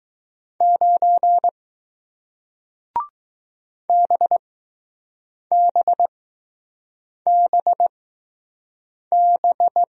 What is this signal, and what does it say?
Morse code, heard as a keyed tone, about 20 wpm, spaced as for 6 wpm, 700 Hz. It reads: 9 BBBB